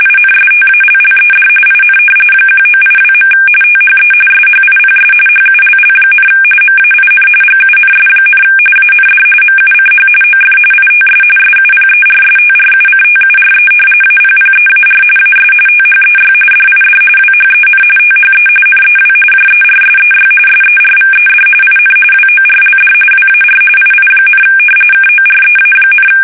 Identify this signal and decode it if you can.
RTTY, about 45.45 baud. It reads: INSTALLED IN 1982. THE SHIPS HULL, MAIN GUNS, AND MUCH OF ITS SUPERSTRUCTURE APPEAR AS THEY DID WHEN THE NEW JERSEY SERVED IN WORLD WAR II, KOREA,
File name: fsk850 / battleshipnj